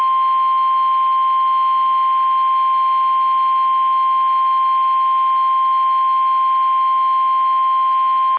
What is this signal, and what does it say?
Emergency Alert System (SAME) tones (the 1050 Hz attention tone). It is heard from 1.1 to 7.2 s.
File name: eas / KUSO / KUSO